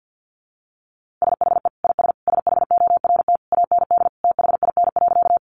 Morse code, heard as a keyed tone, about 50 wpm, 700 Hz. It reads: SHE IS SHORT AND THIN.
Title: Sentences from Top 1200 Words 50wpm